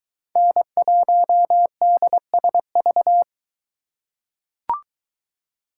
Morse code, heard as a keyed tone, about 22 wpm, 700 Hz. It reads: N1DSV